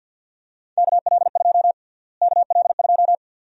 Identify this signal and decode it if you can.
Morse code, heard as a keyed tone, about 51 wpm, 700 Hz. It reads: KL2 KL2
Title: USA Call Sign Prefixes - Repeated 2x - 50wpm